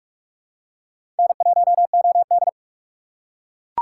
Morse code, heard as a keyed tone, about 43 wpm, 700 Hz. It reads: N1OD